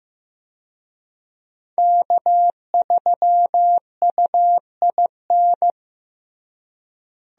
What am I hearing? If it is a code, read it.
Morse code, heard as a keyed tone, about 15 wpm, 700 Hz. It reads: K3UIN
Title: USA Call Signs - 5 Characters - Repeated 3x - 15wpm